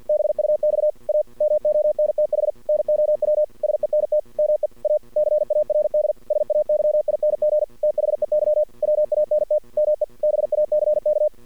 Morse code, heard as a keyed tone, about 52 wpm, 610 Hz. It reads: BUX A MONTH NOW SENT ME A CARD SAYING THEY WANT ME BACK
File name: archived / CRSnet